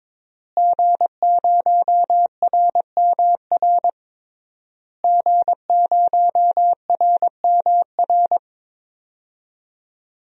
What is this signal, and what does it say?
Morse code, heard as a keyed tone, about 22 wpm, 700 Hz. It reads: G0RMR G0RMR
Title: DXCC-100 Call Signs - 5 Characters - Repeated 3x - 22wpm